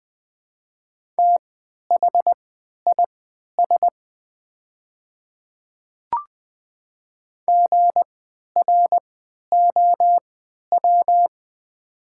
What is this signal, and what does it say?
Morse code, heard as a keyed tone, about 20 wpm, spaced as for 11 wpm, 700 Hz. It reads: THIS GROW
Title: 3x Character Spacing - Top 500 Words - 20wpm